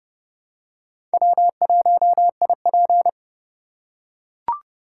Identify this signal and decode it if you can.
Morse code, heard as a keyed tone, about 31 wpm, 700 Hz. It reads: W1IP
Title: USA Call Signs - 4 Characters - 4x Character Spacing - 30wpm